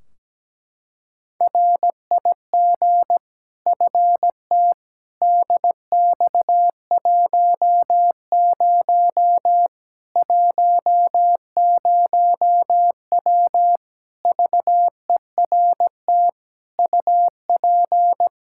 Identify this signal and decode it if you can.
Morse code, heard as a keyed tone, about 17 wpm, 700 Hz. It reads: RIG FT DX10 10W VERT UP
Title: Practice QSO - Exchange 2 - Info - 17wpm